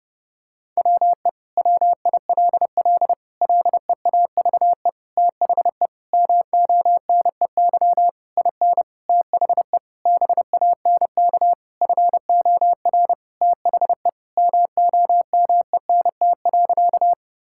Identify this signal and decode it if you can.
Morse code, heard as a keyed tone, about 30 wpm, 700 Hz. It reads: WE WILL LEAVE THE MONEY IN THE BANK FOR THE MOMENT.
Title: Sentences from Top 1000 Words - Review Part 6 - 30wpm